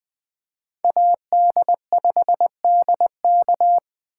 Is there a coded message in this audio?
Yes: Morse code from 0.8 to 3.8 s, heard as a keyed tone, about 21 wpm, 700 Hz. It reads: AD5DK